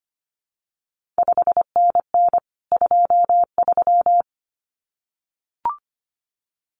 Morse code, heard as a keyed tone, about 25 wpm, 700 Hz. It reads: 5NN 23
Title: CQ WW DX Contest Exchange Encore - 25wpm